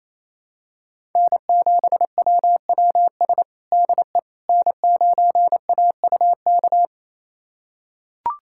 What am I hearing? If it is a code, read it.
Morse code, heard as a keyed tone, about 28 wpm, 700 Hz. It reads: N7WWS DE N9AUK
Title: USA Callsign DE Callsign - 5 Characters - 28wpm